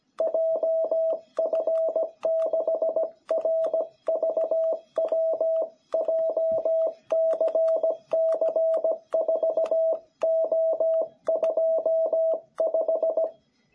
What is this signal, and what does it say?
Morse code, heard as a keyed tone, about 17 wpm, 660 Hz. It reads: JF6RVW JCC4O25
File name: jh8cbh / 2021cw